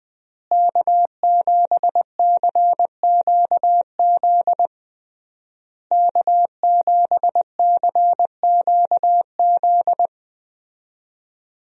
Morse code, heard as a keyed tone, about 20 wpm, 700 Hz. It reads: K7CQZ K7CQZ
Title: USA Call Signs - 5 Characters - Repeated 3x - 20wpm